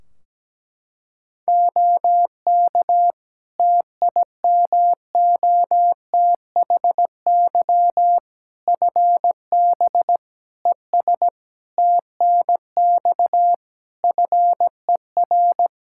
Morse code, heard as a keyed tone, about 17 wpm, 700 Hz. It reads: OK TIMOTHY FB ES TNX FER
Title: K1USN SST QSO Practice 17wpm